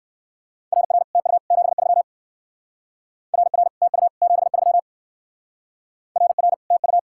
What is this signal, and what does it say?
Morse code, heard as a keyed tone, about 59 wpm, 700 Hz. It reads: RR TU 73 RR TU 73 RR TU